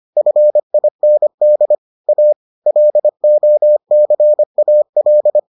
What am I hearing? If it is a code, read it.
Morse code, heard as a keyed tone, about 25 wpm, 600 Hz. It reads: FIND A LOCAL